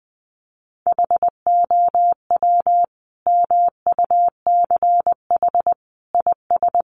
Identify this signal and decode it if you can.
Morse code, heard as a keyed tone, about 20 wpm, 700 Hz. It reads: HOW MUCH IS
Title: Sentences from Top 1100 Words 20wpm